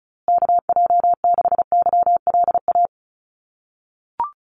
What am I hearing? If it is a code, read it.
Morse code, heard as a keyed tone, about 34 wpm, 700 Hz. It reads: KJ6YLA
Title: USA Call Signs Repeated 2x 35wpm